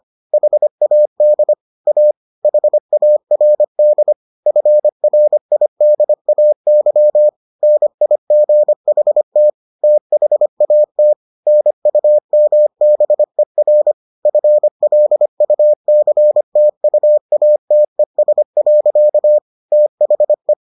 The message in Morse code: HAD A HARD FRIDAY NIGHT THAT NUMBER FLUCTUATES. THE

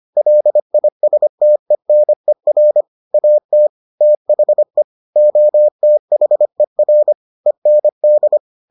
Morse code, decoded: LISTENER AT THE OTHER END

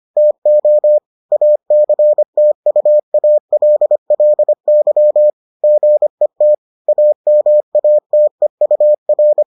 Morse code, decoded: TO ACTUALLY GET AMATEUR

